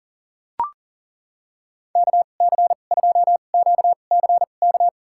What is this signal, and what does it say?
Morse code, heard as a keyed tone, about 40 wpm, spaced as for 29 wpm, 700 Hz. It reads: KC2QCK
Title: USA Call Signs - 6 Characters - 2x Character Spacing - 40wpm